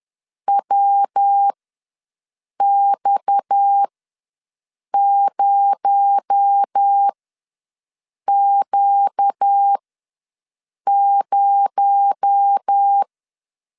Morse code, heard as a keyed tone, about 10 wpm, spaced as for 6 wpm, 800 Hz. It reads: WX0Q0
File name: LEZIONI / 6 PKX0 / 6 PKX0 ws 10wpm 4220